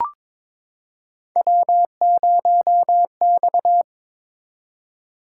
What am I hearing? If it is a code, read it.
Morse code, heard as a keyed tone, about 23 wpm, 700 Hz. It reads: W0X
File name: practice / 22wpm / Three Letter-Number Combinations - 22wpm